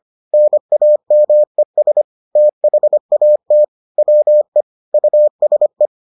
Morse code, heard as a keyed tone, about 25 wpm, 600 Hz. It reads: NAMES THAT WE USE